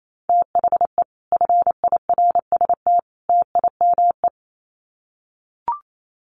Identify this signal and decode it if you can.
Morse code, heard as a keyed tone, about 28 wpm, 700 Hz. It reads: THE FIRST TIME
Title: Sets of 3 Words 28wpm